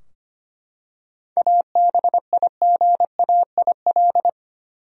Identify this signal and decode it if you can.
Morse code, heard as a keyed tone, about 25 wpm, 700 Hz. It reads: ABIGAIL
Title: USA Common Names - 25wpm